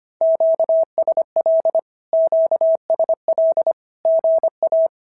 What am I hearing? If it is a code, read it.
Morse code, heard as a keyed tone, about 25 wpm, 650 Hz. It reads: QSL QSL GA
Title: SOTA QSO Practice 25wpm